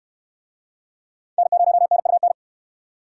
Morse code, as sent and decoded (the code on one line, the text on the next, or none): -. ----- -. ..- -.
N0NUN